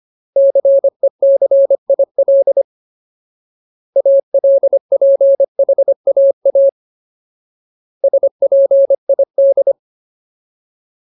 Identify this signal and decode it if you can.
Morse code, heard as a keyed tone, about 25 wpm, 550 Hz. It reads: CECIL ALPHAA SPID
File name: Ultimate Word List / 25WPM Ultimate